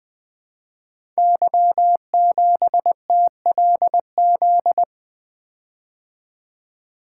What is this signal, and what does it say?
Morse code, heard as a keyed tone, about 20 wpm, 700 Hz. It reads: Y7TLZ
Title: Code Groups - 5 Letter:Numbers - 20wpm